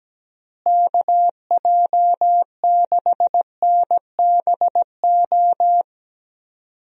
Morse code, heard as a keyed tone, about 17 wpm, 700 Hz. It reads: KJ6NBO
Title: USA Call Signs - 6 Characters - Repeated 2x - 17wpm